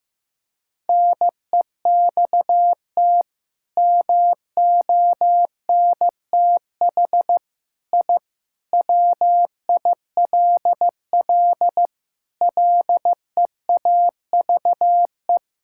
Morse code, heard as a keyed tone, about 16 wpm, 700 Hz. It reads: NEXT MONTH I WILL LEAVE
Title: Sentences from Top 1000 Words - Review Part 5 - 15wpm